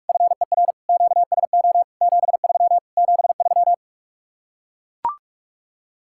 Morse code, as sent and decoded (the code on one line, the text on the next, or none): ..-. .-. / --.- ... --- / --... ...-- / --... ...--
FER QSO 73 73